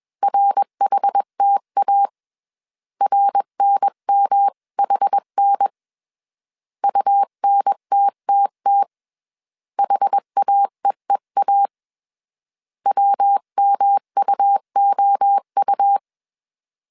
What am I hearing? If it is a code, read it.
Morse code, heard as a keyed tone, about 21 wpm, 800 Hz. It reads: RHTA RNMHN UNTTT HAEEA WMUOU
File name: CW / DET / 1 3 1500C S B 20wpm 1754